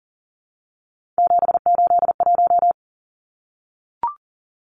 Morse code, heard as a keyed tone, about 40 wpm, 700 Hz. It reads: 781
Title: Numbers - 3-Digit 40wpm